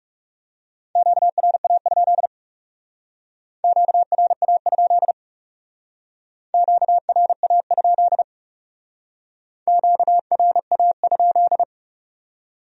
Morse code, heard as a keyed tone, about 40 wpm, 700 Hz. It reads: QRA? QRA? QRA? QRA?